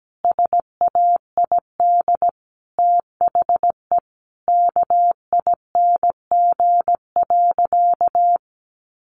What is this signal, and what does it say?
Morse code, heard as a keyed tone, about 17 wpm, 700 Hz. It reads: SAID THE KING.
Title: Sentences from Top 400 Words 17wpm